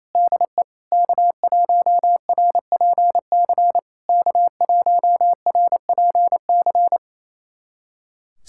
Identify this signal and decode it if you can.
Morse code, heard as a keyed tone, about 28 wpm, 690 Hz. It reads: DE K1RPC K1RPC
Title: SOTA QSO Practice 28wpm